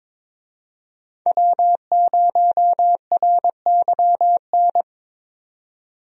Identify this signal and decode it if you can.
Morse code, heard as a keyed tone, about 22 wpm, 700 Hz. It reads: W0RYN
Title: USA Call Signs - 5 Characters - 22wpm